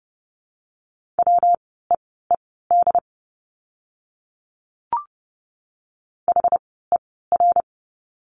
Morse code, heard as a keyed tone, about 30 wpm, spaced as for 17 wpm, 700 Hz. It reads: WEED HER
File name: course / 30:10wpm / 115 Lesson 19f - Letter w - Words 30wpm